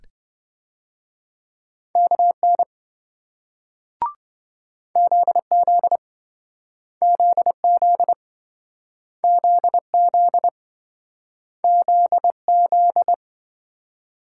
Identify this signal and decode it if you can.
Morse code, heard as a keyed tone, about 28 wpm, 700 Hz. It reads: KN ZZ ZZ ZZ ZZ